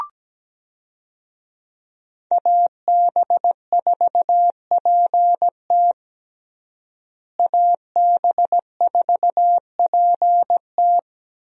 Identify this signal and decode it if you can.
Morse code, heard as a keyed tone, about 17 wpm, 700 Hz. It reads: AB4PT AB4PT
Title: USA Call Signs - 5 Characters - Repeated 3x - 17wpm